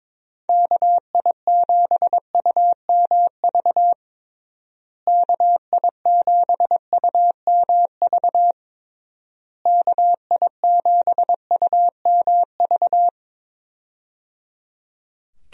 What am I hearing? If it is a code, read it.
Morse code, heard as a keyed tone, about 22 wpm, 700 Hz. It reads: KI7UMV KI7UMV KI7UMV
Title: USA Call Signs - 6 Characters - Repeated 3x - 22wpm